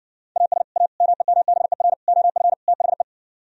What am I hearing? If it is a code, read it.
Morse code, heard as a keyed tone, about 70 wpm, 700 Hz. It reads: AS A MEMBER OF THE